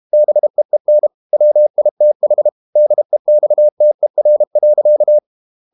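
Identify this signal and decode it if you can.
Morse code, heard as a keyed tone, about 32 wpm, 600 Hz. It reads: BEEN WITH DEXTER.
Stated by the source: Morse code practice at 32 wpm from CW med Gnister, reading Ken Follett's novel Never in English.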